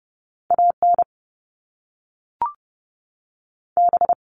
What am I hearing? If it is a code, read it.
Morse code, heard as a keyed tone, about 30 wpm, 700 Hz. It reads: A N B
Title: Top Two Letter Combinations in Words 30wpm